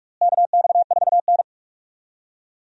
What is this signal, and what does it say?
Morse code, heard as a keyed tone, about 42 wpm, 700 Hz. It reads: KX4N
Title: USA Call Signs - 4 Characters - Repeated 2x - 45wpm